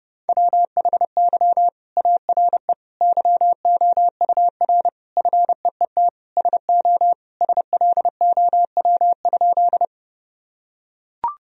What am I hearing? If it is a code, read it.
Morse code, heard as a keyed tone, about 30 wpm, 700 Hz. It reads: WHY ARE YOUR FEET SO SLOW?
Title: Sentences from Top 1000 Words - Review Part 2 - 30wpm